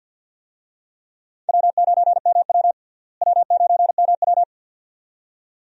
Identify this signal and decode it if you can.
Morse code, heard as a keyed tone, about 47 wpm, 700 Hz. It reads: W9MW W9MW
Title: USA Call Signs - 4 Characters - Repeated 2x - 50wpm